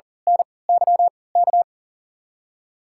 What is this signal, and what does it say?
Morse code, heard as a keyed tone, about 38 wpm, spaced as for 22 wpm, 700 Hz. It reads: NYK
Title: USA Call Signs - 5 Characters - 3x Character Spacing - 40wpm